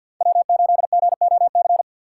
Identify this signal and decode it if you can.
Morse code, heard as a keyed tone, about 48 wpm, 700 Hz. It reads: W8GOC
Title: USA Call Signs - 5 Characters - 6x Character Spacing - 50wpm